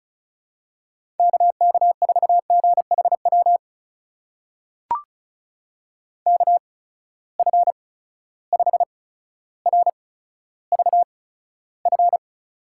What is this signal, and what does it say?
Morse code, heard as a keyed tone, about 36 wpm, spaced as for 15 wpm, 700 Hz. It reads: KK4GHW KF5RVF
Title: USA Call Signs - 6 Characters - 8x Character Spacing - 35wpm